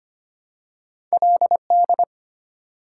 Morse code, heard as a keyed tone, about 25 wpm, 700 Hz. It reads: LD
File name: Two Letter-Number Combinations - 25wpm